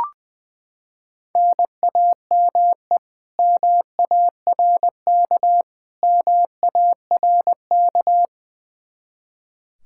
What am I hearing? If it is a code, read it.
Morse code, heard as a keyed tone, about 20 wpm, 700 Hz. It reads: NAME MARK MARK
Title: Practice QSO - Exchange 1 - Beginning - 20wpm